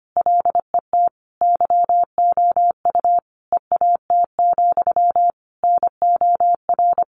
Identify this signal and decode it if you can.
Morse code, heard as a keyed tone, about 25 wpm, 700 Hz. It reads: LET YOU EAT, NOR
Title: Sentences from Top 1500 Words 25wpm